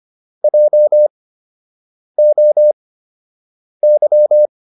Morse code, heard as a keyed tone, about 25 wpm, spaced as for 7 wpm, 600 Hz. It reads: JOY